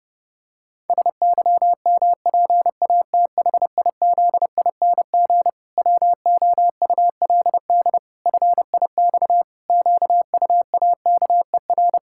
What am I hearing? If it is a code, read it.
Morse code, heard as a keyed tone, about 30 wpm, 700 Hz. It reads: SYMPATHIZING WOULD FIX QUAKER
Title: Pangram Challenge v2 - 30wpm